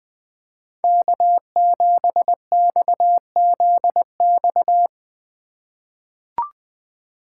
Morse code, heard as a keyed tone, about 20 wpm, 700 Hz. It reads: K7XZX
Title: USA Call Signs - 5 Characters - 4x Character Spacing - 20wpm